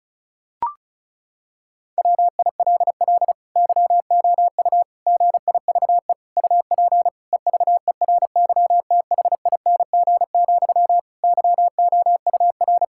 Morse code, heard as a keyed tone, about 36 wpm, 700 Hz. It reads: WILL YOU GIVE UP EVERYTHING, YOUR
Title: Sentences from Top 1000 Words - Review Part 5 - 35wpm